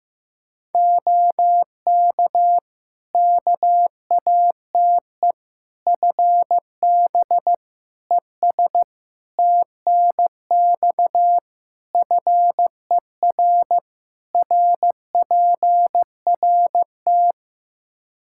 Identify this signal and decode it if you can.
Morse code, heard as a keyed tone, about 15 wpm, 700 Hz. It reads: OK KATE FB ES TNX FER RPRT
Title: Practice QSO - Exchange 2 - Info - 15wpm